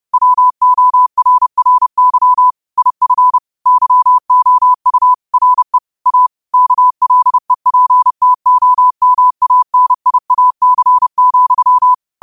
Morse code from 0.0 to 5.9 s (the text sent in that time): WORRY IF YOU RE